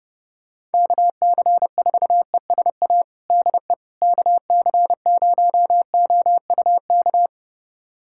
Morse code, heard as a keyed tone, about 32 wpm, 700 Hz. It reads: KC4ESA DE KC0OUK
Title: USA Callsign DE Callsign - 6 Characters - 30wpm